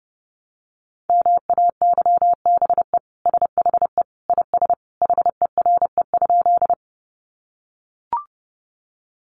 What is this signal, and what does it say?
Morse code, heard as a keyed tone, about 30 wpm, 700 Hz. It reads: MAYBE SHE IS HERE?
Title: Sentences from Top 1000 Words 30wpm